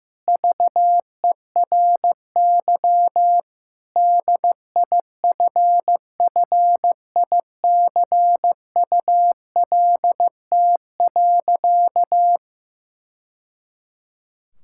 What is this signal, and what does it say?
Morse code, heard as a keyed tone, about 15 wpm, 700 Hz. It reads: VERY DIFFICULT.
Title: Sentences from Top 1000 Words - Review Part 6 - 15wpm